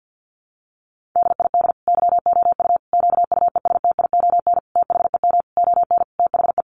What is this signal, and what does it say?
Morse code, heard as a keyed tone, about 50 wpm, 700 Hz. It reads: DID YOU QUESTION THEM ON THE
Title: Sentences from Top 800 Words 50wpm